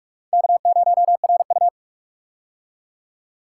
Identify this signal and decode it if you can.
Morse code, heard as a keyed tone, about 44 wpm, 700 Hz. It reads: K0RU
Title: K1USN SST QSO Practice - 45wpm